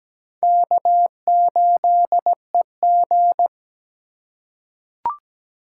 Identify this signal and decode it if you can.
Morse code, heard as a keyed tone, about 17 wpm, 700 Hz. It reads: K8EG
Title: USA Call Signs - 4 Characters - 4x Character Spacing - 17wpm